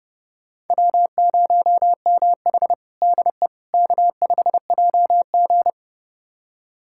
Morse code, heard as a keyed tone, about 30 wpm, 700 Hz. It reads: W0MH DE K5JG
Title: USA Callsign DE Callsign - 4 Characters - 30wpm